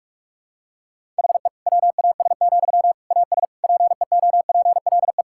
Morse code, heard as a keyed tone, about 46 wpm, 700 Hz. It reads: HE WAS, AS PEOPLE
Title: Sentences from Top 1000 Words - Review Part 6 - 45wpm